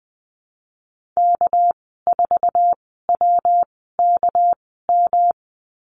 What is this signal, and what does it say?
Morse code, heard as a keyed tone, about 19 wpm, spaced as for 14 wpm, 700 Hz. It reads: K4WKM
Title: 200 Lesson 32g - Letter k - Call Signs 20wpm